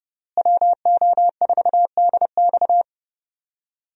Morse code, heard as a keyed tone, about 31 wpm, 700 Hz. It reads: WO4DX
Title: USA Call Signs - 5 Characters - Repeated 2x - 30wpm